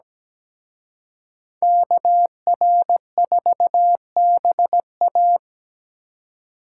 Morse code, heard as a keyed tone, about 20 wpm, 700 Hz. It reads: KR4BA
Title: Speed Racing - USA 2x2 Call Signs 17wpm